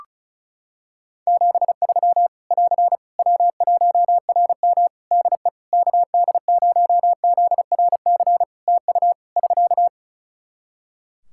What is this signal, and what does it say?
Morse code, heard as a keyed tone, about 36 wpm, 700 Hz. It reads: 73 <AR> W1RM DE KD0ZRC TU <SK>
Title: Practice QSO - Exchange 4 - Ending - 35wpm